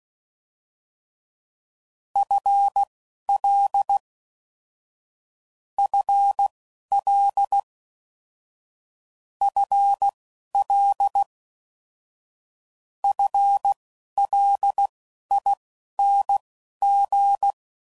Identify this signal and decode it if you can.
Morse code, heard as a keyed tone, about 16 wpm, spaced as for 11 wpm, 800 Hz. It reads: FL FL FL FLING
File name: Basic Sets / QST de WA7RSO / Combnations